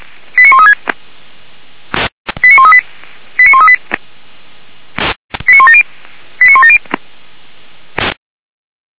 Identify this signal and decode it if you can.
Five-tone selective call (ZVEI), heard as quick runs of five tones, tones of about 70 ms. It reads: 89136 89138 89138 89170 89170